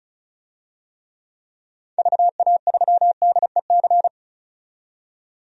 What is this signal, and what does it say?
Morse code, heard as a keyed tone, about 36 wpm, 700 Hz. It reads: VA3DEC
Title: DXCC-100 Call Signs - 6 Characters - 35wpm